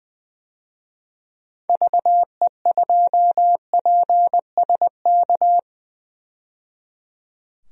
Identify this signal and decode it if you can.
Morse code, heard as a keyed tone, about 20 wpm, 700 Hz. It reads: VE2PSK
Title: DXCC-100 Call Signs - 6 Characters - 20wpm